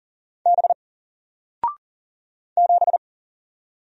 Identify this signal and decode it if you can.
Morse code, heard as a keyed tone, about 40 wpm, 700 Hz. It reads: B 7
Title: Single Letter-Number 40wpm